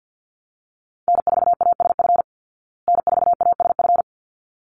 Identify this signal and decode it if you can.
Morse code, heard as a keyed tone, about 50 wpm, 700 Hz. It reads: N4ASF N4ASF
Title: Call Signs Repeated 3x - 50wpm